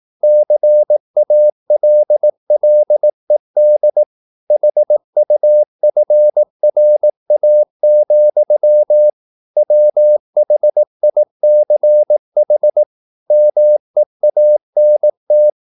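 Morse code, heard as a keyed tone, about 18 wpm, 600 Hz. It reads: CALLED HUFRA, WHICH MEANT
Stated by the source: Morse practice at 18 wpm, English text, from CW med Gnister.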